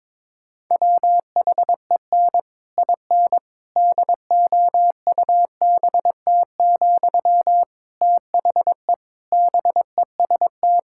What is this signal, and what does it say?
Morse code, heard as a keyed tone, about 22 wpm, 700 Hz. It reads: WHEN IN DOUBT, THE BEST
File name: Difficult Sentences with Hints - 22wpm